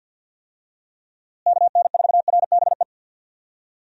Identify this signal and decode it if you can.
Morse code, heard as a keyed tone, about 51 wpm, 700 Hz. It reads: KN4RBE
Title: USA Call Signs - 6 Characters - Repeated 3x - 50wpm